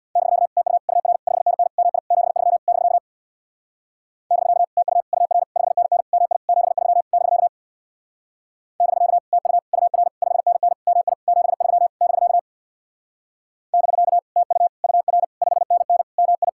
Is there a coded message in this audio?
Yes: Morse code from 0.2 to 16.5 s, heard as a keyed tone, about 65 wpm, 700 Hz. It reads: <BK> TU UR 5NN MI 73 <BK> <BK> TU UR 5NN MI 73 <BK> <BK> TU UR 5NN MI 73 <BK> <BK> TU UR 5NN MI